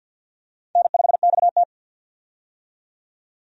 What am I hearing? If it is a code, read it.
Morse code, heard as a keyed tone, about 50 wpm, 700 Hz. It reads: N5XT
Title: USA Call Signs - 4 Characters - 50wpm